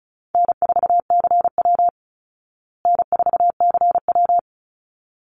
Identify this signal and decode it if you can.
Morse code, heard as a keyed tone, about 34 wpm, 700 Hz. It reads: N4CW N4CW
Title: Easy Call Signs Repeated 2x - 35wpm